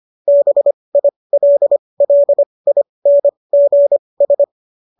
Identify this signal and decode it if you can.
Morse code, heard as a keyed tone, about 25 wpm, spaced as for 20 wpm, 570 Hz. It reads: BILLINGS